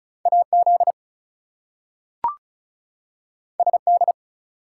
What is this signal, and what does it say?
Morse code, heard as a keyed tone, about 36 wpm, 700 Hz. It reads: AZ SD
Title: US States and Canadian Provinces 35wpm